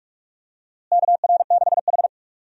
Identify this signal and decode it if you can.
Morse code, heard as a keyed tone, about 46 wpm, 700 Hz. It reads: KR6H
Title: USA Call Signs - 4 Characters - Repeated 2x - 45wpm